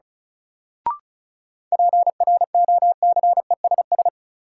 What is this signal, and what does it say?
Morse code, heard as a keyed tone, about 36 wpm, 700 Hz. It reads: PROCESS
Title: Sentences from Top 1000 Words - Review Part 3 - New Word Drill - 35wpm